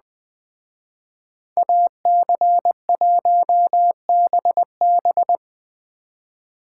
Morse code, heard as a keyed tone, about 20 wpm, 700 Hz. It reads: AC1BB